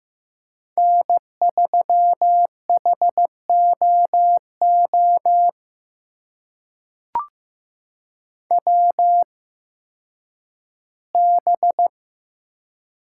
Morse code, heard as a keyed tone, about 15 wpm, spaced as for 10 wpm, 700 Hz. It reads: N3HOO WB
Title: USA Call Signs - 8x Character Spacing - 15wpm